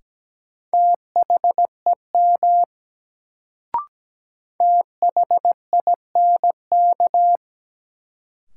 Morse code, heard as a keyed tone, about 17 wpm, 700 Hz. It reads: THEM THINK
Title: Sets of 2 Words - New Word Drill - 17wpm